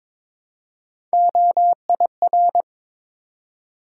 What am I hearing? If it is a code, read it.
Morse code, heard as a keyed tone, about 22 wpm, 700 Hz. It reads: OIR
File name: practice / 22wpm / Three Letter-Number Combinations - 22wpm